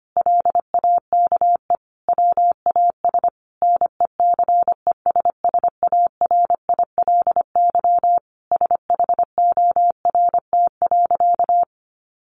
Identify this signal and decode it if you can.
Morse code, heard as a keyed tone, about 25 wpm, 700 Hz. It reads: LAKE WAS NECESSARILY SHORT.
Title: Sentences from Top 1400 Words 25wpm